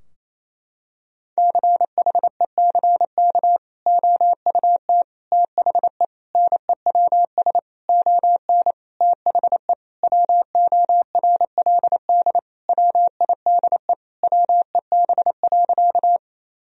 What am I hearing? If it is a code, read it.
Morse code, heard as a keyed tone, about 28 wpm, 700 Hz. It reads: CHECK OUT THE NEWS ON THE WORLD WIDE WEB.
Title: Sentences from Top 1000 Words - Review Part 4 - 28wpm